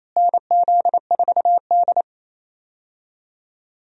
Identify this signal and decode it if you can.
Morse code, heard as a keyed tone, about 28 wpm, 700 Hz. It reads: NZ4D
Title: USA Call Signs - 4 Characters - Repeated 2x - 28wpm